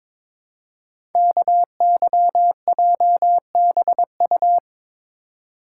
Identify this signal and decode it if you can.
Morse code, heard as a keyed tone, about 22 wpm, 700 Hz. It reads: KYJBU
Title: Code Groups - 5 Letters - 22wpm